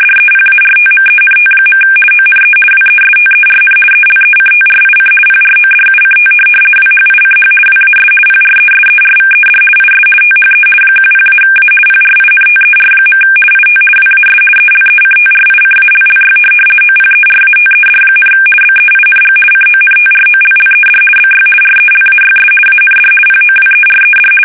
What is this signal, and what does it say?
RTTY, about 50 baud. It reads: DEPARTMENT. ONE WESTERN ELECTRIC PRODUCT WHICH WAS PRESERVED IS THE 10-A PRINTER. IT WAS ELECTRICALLY OPERATED, BEING CONTROLLED BY RELAYS AND POWERED
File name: fsk850-50B / teletype-corp